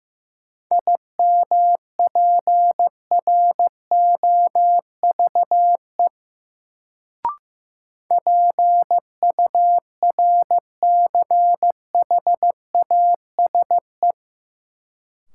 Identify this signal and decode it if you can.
Morse code, heard as a keyed tone, about 15 wpm, 700 Hz. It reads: IMPROVE PURCHASE
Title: Sentences from Top 800 Words - New Word Drill - 15wpm